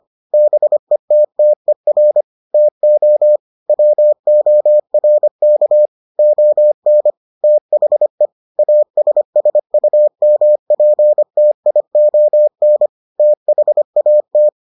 Morse code, heard as a keyed tone, about 25 wpm, 600 Hz. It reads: BETTER TO WORK ON THE ASSUMPTION THAT